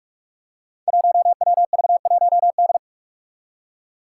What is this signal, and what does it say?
Morse code, heard as a keyed tone, about 47 wpm, 700 Hz. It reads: 1WV1D
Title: Five Challenging Letter-Number Combinations 45wpm